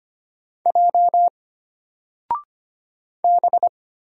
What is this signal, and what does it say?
Morse code, heard as a keyed tone, about 25 wpm, 700 Hz. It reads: J B
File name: Single-Letter-Number-25wpm